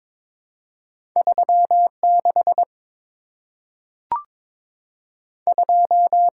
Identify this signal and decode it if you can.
Morse code, heard as a keyed tone, about 22 wpm, 700 Hz. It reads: 36 2
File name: practice / 22wpm / Two Letter-Number Combinations - 22wpm